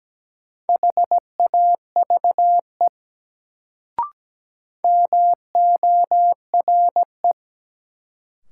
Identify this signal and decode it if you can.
Morse code, heard as a keyed tone, about 17 wpm, 700 Hz. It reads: HAVE MORE
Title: Sentences from Top 100 Words - New Word Drill - 17wpm